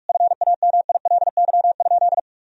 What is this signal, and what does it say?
Morse code, heard as a keyed tone, about 47 wpm, 700 Hz. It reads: FAMILY?
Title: Sentences from Top 1000 Words - Review Part 4 - 45wpm